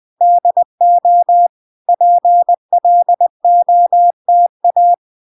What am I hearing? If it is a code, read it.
Morse code, heard as a keyed tone, about 20 wpm, 700 Hz. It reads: DO PLOTA